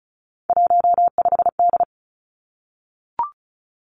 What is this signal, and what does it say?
Morse code, heard as a keyed tone, about 35 wpm, 700 Hz. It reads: 15D
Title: Three Challenging Letter-Number Combinations 35wpm